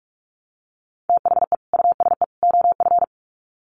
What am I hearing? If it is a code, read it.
Morse code, heard as a keyed tone, about 44 wpm, 700 Hz. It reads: THE USE OF
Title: Sets of 3 Words 45wpm